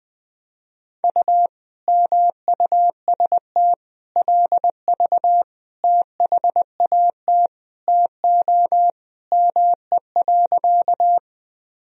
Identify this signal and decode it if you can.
Morse code, heard as a keyed tone, about 20 wpm, 700 Hz. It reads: U MUST LV THAT TO ME.
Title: Sentences from Top 400 Words and Abbreviations 20wpm